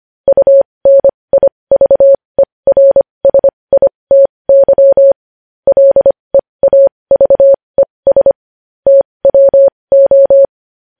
Morse code, heard as a keyed tone, about 25 wpm, spaced as for 20 wpm, 550 Hz. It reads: UNIVERSITY LEAVES TWO
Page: News Headlines in Morse Code at 20 WPM